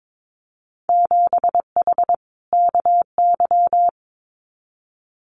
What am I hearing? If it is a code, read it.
Morse code, heard as a keyed tone, about 22 wpm, 700 Hz. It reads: 7H KY
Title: Winter Field Day Exchanges Encore 22wpm